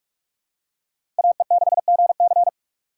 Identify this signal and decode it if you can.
Morse code, heard as a keyed tone, about 42 wpm, 700 Hz. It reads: AE6GC